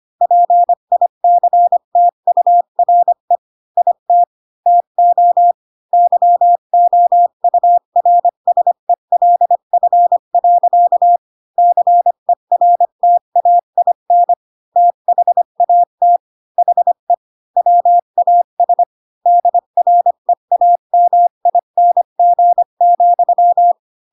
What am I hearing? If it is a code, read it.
Morse code, heard as a keyed tone, about 25 wpm, 700 Hz. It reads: PICTURE IT TO YOURSELF. CERTAIN THAT HE WAS DREAMING,
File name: Pinocchio / Pinocchio ch5 20 25 ik1hge